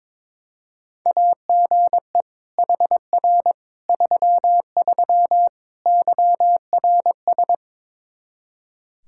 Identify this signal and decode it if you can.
Morse code, heard as a keyed tone, about 21 wpm, 700 Hz. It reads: AGE HR 33 YRS
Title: K1USN SST QSO Practice 22wpm